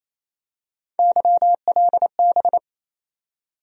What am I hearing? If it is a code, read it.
Morse code, heard as a keyed tone, about 28 wpm, 700 Hz. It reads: YLB
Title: USA Call Sign Postfixes - 3 Letters 28wpm